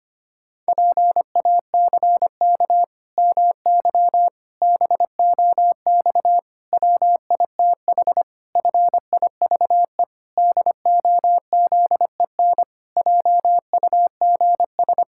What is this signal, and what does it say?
Morse code, heard as a keyed tone, about 25 wpm, 700 Hz. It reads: PACK MY BOX WITH FIVE DOZEN JUGS
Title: Pangram Challenge v2 - 25wpm